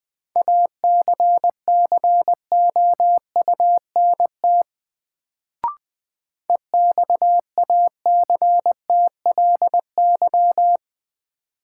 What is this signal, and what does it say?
Morse code, heard as a keyed tone, about 20 wpm, 700 Hz. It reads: ACCOUNT EXACTLY
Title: Sentences from Top 900 Words - New Word Drill - 20wpm